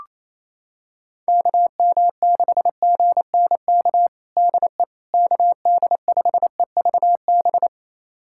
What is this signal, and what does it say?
Morse code, heard as a keyed tone, about 28 wpm, 700 Hz. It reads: KM6GNK DE KD5EVB
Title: USA Callsign DE Callsign - 6 Characters - 28wpm